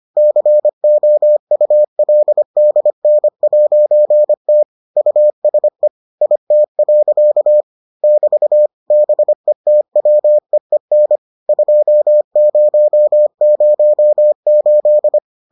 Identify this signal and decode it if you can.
Morse code, heard as a keyed tone, about 25 wpm, 600 Hz. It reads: COULDN'T USE IT. <BT> BETWEEN 2008